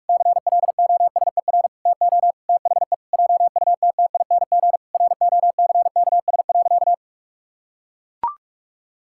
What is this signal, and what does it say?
Morse code, heard as a keyed tone, about 41 wpm, 700 Hz. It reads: CLOSER TO THE JUTTING ROCKS.
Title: Pangram Challenge v2 - 45wpm